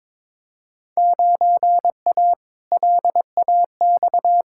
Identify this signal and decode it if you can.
Morse code, heard as a keyed tone, about 22 wpm, 700 Hz. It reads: 9A LAX
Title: Field Day Exchanges Encore - 22wpm